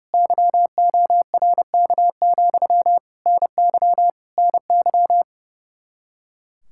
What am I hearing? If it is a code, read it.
Morse code, heard as a keyed tone, about 29 wpm, 700 Hz. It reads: YORK, NY NY
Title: K1USN SST QSO Practice 30wpm